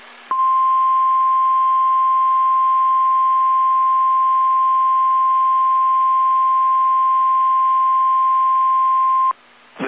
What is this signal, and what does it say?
Emergency Alert System (SAME) tones (the 1050 Hz attention tone). It is heard from 0.3 to 9.3 s.